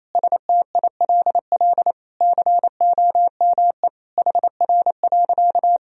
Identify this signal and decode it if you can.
Morse code, heard as a keyed tone, about 28 wpm, 700 Hz. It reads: STILL COME HR.
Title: Sentences from Top 300 Words and Abbreviations 28wpm